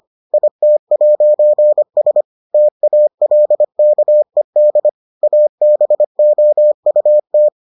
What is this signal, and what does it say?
Morse code, heard as a keyed tone, about 25 wpm, 600 Hz. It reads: IT'S TALKED ABOUT